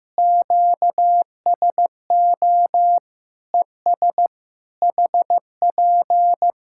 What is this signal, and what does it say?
Morse code, heard as a keyed tone, about 15 wpm, 700 Hz. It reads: QSO ES HP
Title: K1USN SST QSO Practice 15wpm